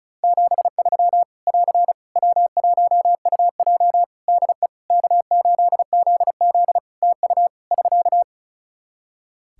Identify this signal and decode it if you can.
Morse code, heard as a keyed tone, about 34 wpm, 700 Hz. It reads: 73 <AR> W1UJ DE K8ZZ TU <SK>
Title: Practice QSO - Exchange 4 - Ending - 35wpm